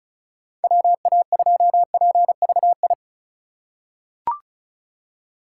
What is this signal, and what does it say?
Morse code, heard as a keyed tone, about 36 wpm, 690 Hz. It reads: WA2PVI
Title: USA Call Signs - 6 Characters - 6x Character Spacing - 35wpm